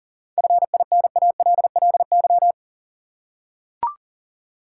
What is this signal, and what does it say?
Morse code, heard as a keyed tone, about 40 wpm, 700 Hz. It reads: FINALLY
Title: Seven Letter Words - 40wpm